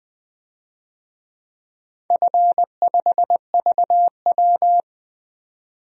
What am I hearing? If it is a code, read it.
Morse code, heard as a keyed tone, about 20 wpm, 700 Hz. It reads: F5VW
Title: DXCC-100 Call Signs - 4 Characters - 20wpm